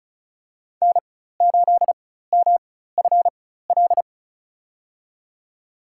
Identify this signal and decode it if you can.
Morse code, heard as a keyed tone, about 36 wpm, spaced as for 16 wpm, 700 Hz. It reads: N8MFL
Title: USA Call Signs - 5 Characters - 4x Character Spacing - 35wpm